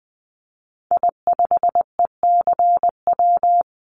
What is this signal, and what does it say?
Morse code, heard as a keyed tone, about 20 wpm, 700 Hz. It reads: I5ECW
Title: Call Signs - 20wpm